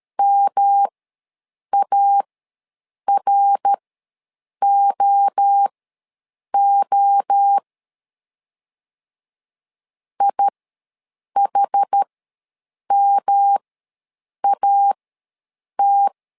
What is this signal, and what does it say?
Morse code, heard as a keyed tone, about 13 wpm, spaced as for 7 wpm, 800 Hz. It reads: MAROO IHMAT
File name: LEZIONI / 3 MWRO / 3 MWRO ws 12wpm 1914